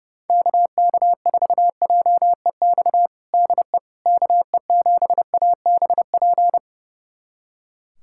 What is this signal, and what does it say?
Morse code, heard as a keyed tone, about 30 wpm, 700 Hz. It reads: KK4JEX DE KE7ABP
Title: USA Callsign DE Callsign - 6 Characters - 30wpm